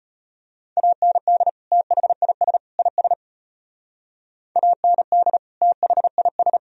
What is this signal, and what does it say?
Morse code, heard as a keyed tone, about 30 wpm, 700 Hz. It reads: AND THIS IS AND THIS
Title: Speed-Racing - Sets of 3 Words 25wpm